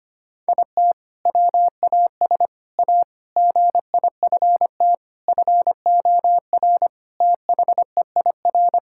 Morse code, heard as a keyed tone, about 25 wpm, 700 Hz. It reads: IT WAS A GIFT FOR THEIR
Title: Sentences from Top 1000 Words - Review Part 6 - 25wpm